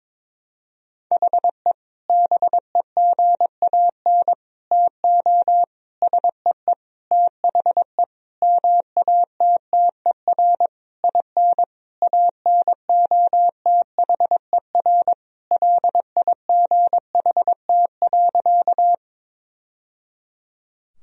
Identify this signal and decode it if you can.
Morse code, heard as a keyed tone, about 22 wpm, 700 Hz. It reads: HE BEGAN TO SEE THE MATTER IN ANOTHER LIGHT.
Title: Sentences from Top 1000 Words - Review Part 1 - 22wpm